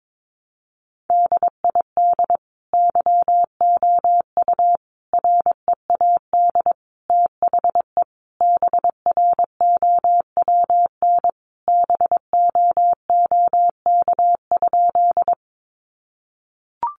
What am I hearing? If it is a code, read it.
Morse code, heard as a keyed tone, about 22 wpm, 700 Hz. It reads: DID YOU READ THE BROWN BOOK?
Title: Sentences from Top 1200 Words 22wpm